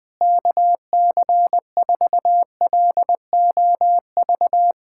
Morse code, heard as a keyed tone, about 20 wpm, 700 Hz. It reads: KC4LOV
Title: USA Callsign DE Callsign 20wpm